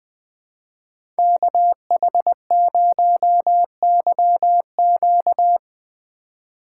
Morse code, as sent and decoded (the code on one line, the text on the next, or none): -.- .... ----- -.-- --.-
KH0YQ